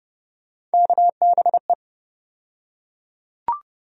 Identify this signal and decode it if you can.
Morse code, heard as a keyed tone, about 30 wpm, 700 Hz. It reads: KBE
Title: USA Call Sign Postfixes - 3 Letters 30wpm